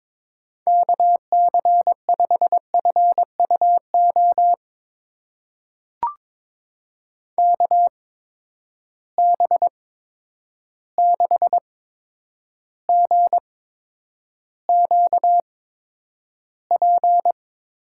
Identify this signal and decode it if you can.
Morse code, heard as a keyed tone, about 22 wpm, spaced as for 9 wpm, 700 Hz. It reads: KC5FUO KB6GQP
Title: USA Call Signs - 6 Characters - 8x Character Spacing - 22wpm